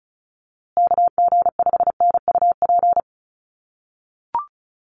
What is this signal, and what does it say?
Morse code, heard as a keyed tone, about 34 wpm, 700 Hz. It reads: KG5NUP
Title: USA Call Signs Repeated 2x 35wpm